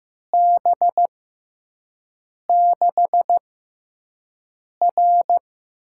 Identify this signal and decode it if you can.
Morse code, heard as a keyed tone, about 15 wpm, spaced as for 5 wpm, 700 Hz. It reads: B6R
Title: USA Call Signs - 6 Characters - 6x Character Spacing - 15wpm